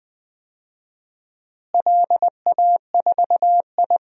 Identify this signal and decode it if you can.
Morse code, heard as a keyed tone, about 20 wpm, 700 Hz. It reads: LA4I
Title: DXCC-100 Call Signs - 4 Characters - Repeated 3x - 20wpm